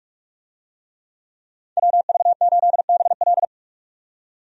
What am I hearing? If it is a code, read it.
Morse code, heard as a keyed tone, about 48 wpm, 700 Hz. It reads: WV8BL
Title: USA Call Signs - 5 Characters - 45wpm